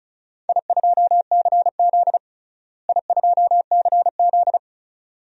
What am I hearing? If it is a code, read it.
Morse code, heard as a keyed tone, about 34 wpm, 700 Hz. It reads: I2CZ I2CZ
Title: DXCC-100 Call Signs - 4 Characters - Repeated 2x - 35wpm